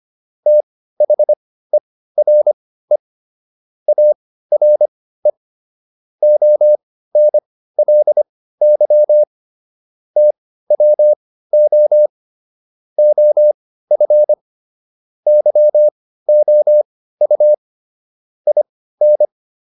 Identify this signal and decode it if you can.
Morse code, heard as a keyed tone, about 25 wpm, spaced as for 15 wpm, 600 Hz. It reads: THERE ARE ONLY TWO OF YOU IN